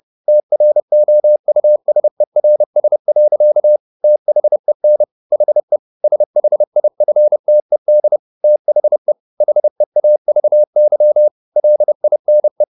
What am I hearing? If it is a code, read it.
Morse code, heard as a keyed tone, about 30 wpm, 600 Hz. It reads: TROUSERS. THEN HE SHIFTED THE HEAVY LINE